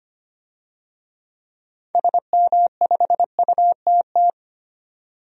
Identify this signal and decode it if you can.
Morse code, heard as a keyed tone, about 25 wpm, 700 Hz. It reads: SM5UTT
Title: DXCC-100 Call Signs - 6 Characters - 25wpm